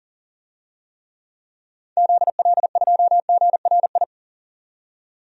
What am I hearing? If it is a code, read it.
Morse code, heard as a keyed tone, about 40 wpm, 700 Hz. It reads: ZL2GRI
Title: DXCC-100 Call Signs - 6 Characters - 40wpm